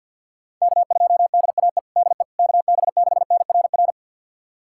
Morse code, heard as a keyed tone, about 51 wpm, 700 Hz. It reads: K2DAE DE KB6NRR
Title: USA Callsign DE Callsign 50wpm